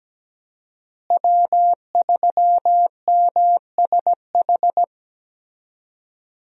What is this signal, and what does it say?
Morse code, heard as a keyed tone, about 17 wpm, 700 Hz. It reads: W3MSH
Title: USA Call Signs - 5 Characters - Repeated 3x - 17wpm